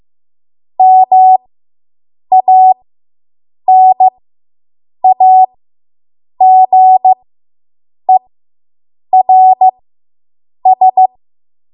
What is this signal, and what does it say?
Morse code, heard as a keyed tone, about 15 wpm, spaced as for 7 wpm, 750 Hz. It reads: MANAGERS